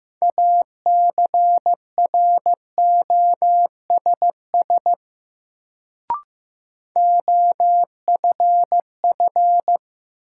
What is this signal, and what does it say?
Morse code, heard as a keyed tone, about 15 wpm, 700 Hz. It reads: ACROSS OFF
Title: Sets of 2 Words - New Word Drill - 15wpm